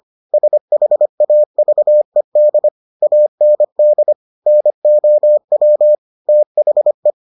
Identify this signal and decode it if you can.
Morse code, heard as a keyed tone, about 25 wpm, 600 Hz. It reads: SHAVED AND NOW THE